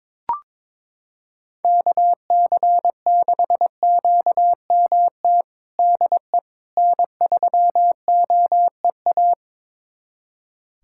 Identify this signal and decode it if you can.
Morse code, heard as a keyed tone, about 22 wpm, 700 Hz. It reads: KC6QMT DE N3OEA
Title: USA Callsign DE Callsign 22wpm